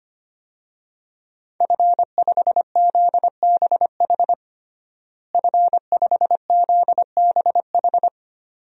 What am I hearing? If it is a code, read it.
Morse code, heard as a keyed tone, about 25 wpm, 700 Hz. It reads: F5ZBH F5ZBH
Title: DXCC-100 Call Signs - 5 Characters - Repeated 2x - 25wpm